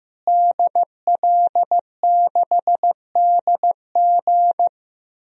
Morse code, heard as a keyed tone, about 15 wpm, 700 Hz. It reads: DL6DG
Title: DXCC-100 Call Signs - 5 Characters - Repeated 3x - 15wpm